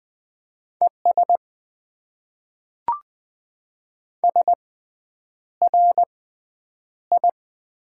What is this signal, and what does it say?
Morse code, heard as a keyed tone, about 20 wpm, spaced as for 9 wpm, 700 Hz. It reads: ES SRI
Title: Top 90 QSO Elements - 6x Character Spacing - 20wpm